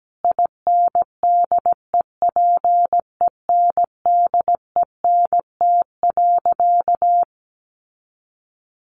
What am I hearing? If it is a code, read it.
Morse code, heard as a keyed tone, about 17 wpm, 700 Hz. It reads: INDEPENDENT.
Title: Sentences from Top 900 Words 17wpm